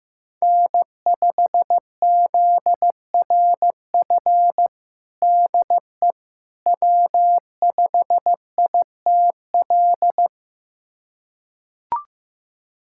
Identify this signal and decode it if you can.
Morse code, heard as a keyed tone, about 15 wpm, 700 Hz. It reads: N5ZRF DE W5ITL
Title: USA Callsign DE Callsign - 5 Characters - 15wpm